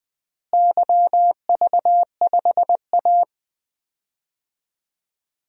Morse code, heard as a keyed tone, about 20 wpm, 700 Hz. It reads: YV5A
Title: DXCC-100 Call Signs - 4 Characters - Repeated 2x - 20wpm